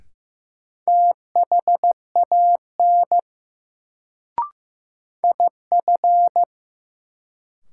Morse code, heard as a keyed tone, about 15 wpm, 700 Hz. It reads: THAN IF
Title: Sentences from Top 100 Words - New Word Drill - 15wpm